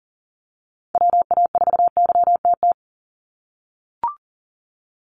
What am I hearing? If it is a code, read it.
Morse code, heard as a keyed tone, about 38 wpm, 700 Hz. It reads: WA4YTT
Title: USA Call Signs 40wpm